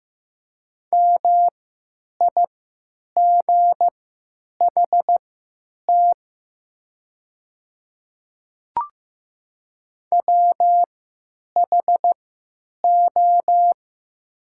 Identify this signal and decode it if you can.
Morse code, heard as a keyed tone, about 15 wpm, spaced as for 9 wpm, 700 Hz. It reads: MIGHT WHO
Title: 3x Character Spacing - Top 300 Words - 15wpm